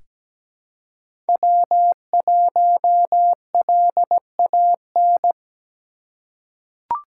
Morse code, heard as a keyed tone, about 17 wpm, 700 Hz. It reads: W1LAN
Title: USA Call Signs - 3x Character Spacing - 17wpm